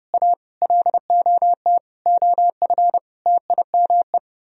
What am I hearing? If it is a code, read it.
Morse code, heard as a keyed tone, about 31 wpm, 700 Hz. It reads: A LOT OF TIME
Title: Speed-Racing - Sets of 5 Words 25wpm